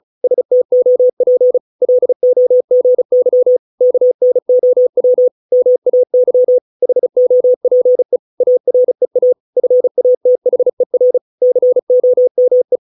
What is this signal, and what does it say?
Morse code, heard as a keyed tone, about 34 wpm, 500 Hz. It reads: STOP LOGY KNOW MAY HOPE AREA FATHER COME